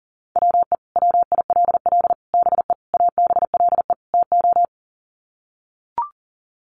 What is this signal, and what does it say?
Morse code, heard as a keyed tone, about 40 wpm, 700 Hz. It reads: WE WILL BE ABLE TO
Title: Sets of 5 Words Encore 40wpm